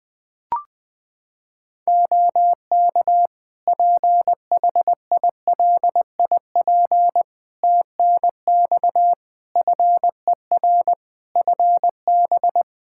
Morse code, heard as a keyed tone, about 20 wpm, 700 Hz. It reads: OK PHILIP TNX FER FB
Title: Practice QSO - Exchange 4 - Ending - 20wpm